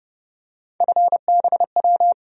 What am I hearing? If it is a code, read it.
Morse code, heard as a keyed tone, about 29 wpm, 700 Hz. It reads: FBW
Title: USA Call Sign Postfixes - 3 Letters 30wpm